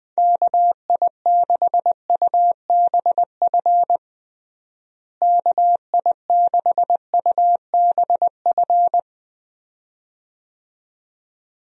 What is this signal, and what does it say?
Morse code, heard as a keyed tone, about 20 wpm, 700 Hz. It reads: KI6UBF KI6UBF
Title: USA Call Signs - 6 Characters - Repeated 2x - 20wpm